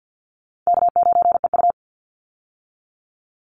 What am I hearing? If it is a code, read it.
Morse code, heard as a keyed tone, about 50 wpm, 700 Hz. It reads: K9EU
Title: Call Signs - 50wpm